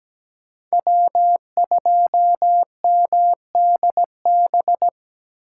Morse code, heard as a keyed tone, about 20 wpm, 700 Hz. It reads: W2MDB
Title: Speed Racing - USA 1x3 Call Signs 17wpm